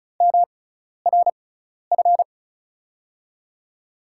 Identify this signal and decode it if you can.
Morse code, heard as a keyed tone, about 36 wpm, spaced as for 12 wpm, 700 Hz. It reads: MRF
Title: USA Call Signs - 6 Characters - 6x Character Spacing - 35wpm